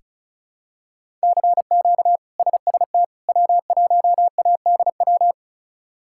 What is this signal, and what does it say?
Morse code, heard as a keyed tone, about 36 wpm, 700 Hz. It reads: CQ SST W1ADW
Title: K1USN SST QSO Practice - 35wpm